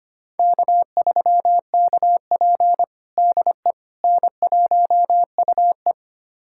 Morse code, heard as a keyed tone, about 25 wpm, 700 Hz. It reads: K3KP DE N1UE
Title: USA Callsign DE Callsign - 4 Characters - 25wpm